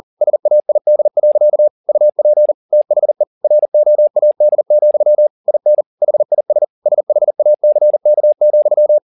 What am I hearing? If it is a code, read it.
Morse code, heard as a keyed tone, about 40 wpm, 600 Hz. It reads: SAID. UP THE ROAD, IN HIS SHACK,
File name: the old man and the sea 2 40wpm